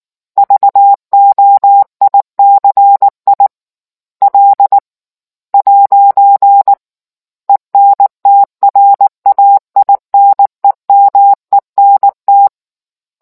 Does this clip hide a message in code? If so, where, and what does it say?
Yes: Morse code from 0.4 to 12.5 s, heard as a keyed tone, about 19 wpm, 800 Hz. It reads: VOICI L'ENTRAINEMENT